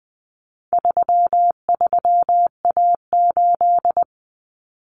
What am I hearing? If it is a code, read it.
Morse code, heard as a keyed tone, about 20 wpm, 700 Hz. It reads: 33A8
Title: Common Cut Numbers - 4-Digit 20wpm